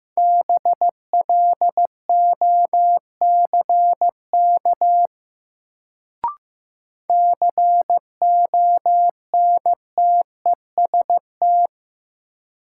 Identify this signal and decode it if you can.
Morse code, heard as a keyed tone, about 16 wpm, 700 Hz. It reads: BLOCK CONTEST
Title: Sentences from Top 1600 Words - New Word Drill - 15wpm